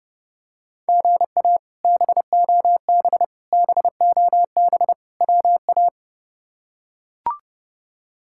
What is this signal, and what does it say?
Morse code, heard as a keyed tone, about 30 wpm, 700 Hz. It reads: GA BOB BOB WA
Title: K1USN SST QSO Practice - 30wpm